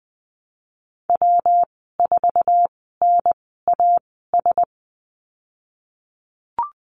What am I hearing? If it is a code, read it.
Morse code, heard as a keyed tone, about 20 wpm, spaced as for 15 wpm, 700 Hz. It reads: W4NAS
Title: 128 Lesson 21g - Number 6 - Call Signs 20wpm